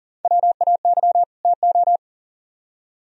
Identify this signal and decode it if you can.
Morse code, heard as a keyed tone, about 50 wpm, 700 Hz. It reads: WAY TO